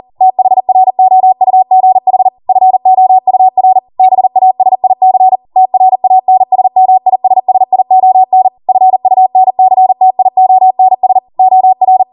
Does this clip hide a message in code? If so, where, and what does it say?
Yes: Morse code from 0.2 to 12.0 s, heard as a keyed tone, about 40 wpm, 750 Hz. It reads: THROUGH FOUR BASIC TRANSMISSION FUNCTIONS OR